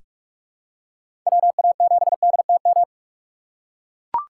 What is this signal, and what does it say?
Morse code, heard as a keyed tone, about 43 wpm, 700 Hz. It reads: WA7DTM
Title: USA Call Signs - 4x Character Spacing - 45wpm